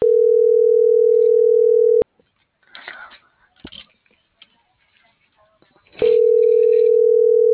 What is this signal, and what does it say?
Ringback tone (North America), 440 + 480 Hz.